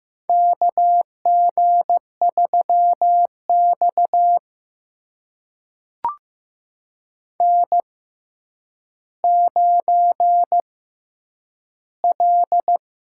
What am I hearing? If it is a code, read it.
Morse code, heard as a keyed tone, about 15 wpm, spaced as for 9 wpm, 700 Hz. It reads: KG3X N9L
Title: USA Call Signs - 4 Characters - 6x Character Spacing - 15wpm